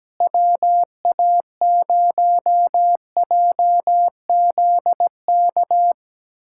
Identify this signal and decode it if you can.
Morse code, heard as a keyed tone, about 17 wpm, 700 Hz. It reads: WA0JZK
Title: K1USN SST QSO Practice - 17wpm